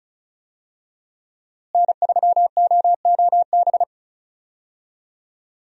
Morse code, heard as a keyed tone, about 36 wpm, 700 Hz. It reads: N3OOB
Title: USA Call Signs - 5 Characters - Repeated 3x - 35wpm